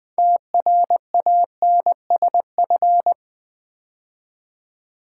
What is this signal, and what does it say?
Morse code, heard as a keyed tone, about 20 wpm, 700 Hz. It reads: TRANSF
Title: Build A Word - 3 Letters At A Time - 9-Letter Words - 20wpm